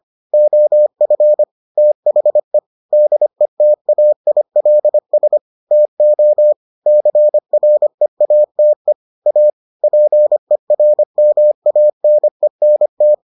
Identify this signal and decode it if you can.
Morse code, heard as a keyed tone, about 25 wpm, 600 Hz. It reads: OF THE DETAILS TO CREATE A PERMANENT